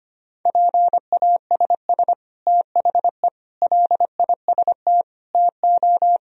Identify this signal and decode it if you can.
Morse code, heard as a keyed tone, about 25 wpm, 700 Hz. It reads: PASS THE LIST TO
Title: Sentences from Top 1000 Words - Review Part 2 - 25wpm